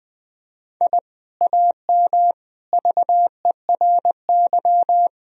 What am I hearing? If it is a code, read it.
Morse code, heard as a keyed tone, about 20 wpm, 700 Hz. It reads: I AM VERY
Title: Sentences from Top 1000 Words - Review Part 2 - 20wpm